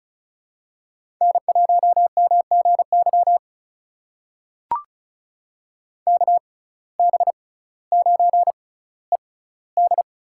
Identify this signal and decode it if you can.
Morse code, heard as a keyed tone, about 36 wpm, spaced as for 18 wpm, 700 Hz. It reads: N1MGY KB9ED
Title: USA Call Signs - 6x Character Spacing - 35wpm